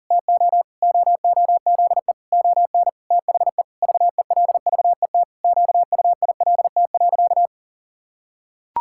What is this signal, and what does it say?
Morse code, heard as a keyed tone, about 37 wpm, 700 Hz. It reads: TO OOZE ON THE VELVET QUILT.
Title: Pangram Challenge v2 - 40wpm